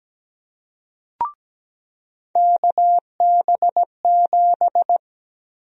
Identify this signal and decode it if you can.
Morse code, heard as a keyed tone, about 17 wpm, 700 Hz. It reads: KB7
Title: USA Call Sign Prefixes - Repeated 2x - 17wpm